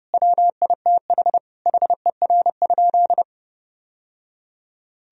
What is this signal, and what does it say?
Morse code, heard as a keyed tone, about 30 wpm, 700 Hz. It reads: WITH HER?
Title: Sentences from Top 1000 Words - Review Part 3 - 30wpm